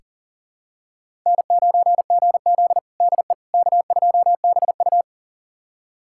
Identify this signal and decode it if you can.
Morse code, heard as a keyed tone, about 40 wpm, 700 Hz. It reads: N9GZ DE K2BU
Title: USA Callsign DE Callsign - 4 Characters - 40wpm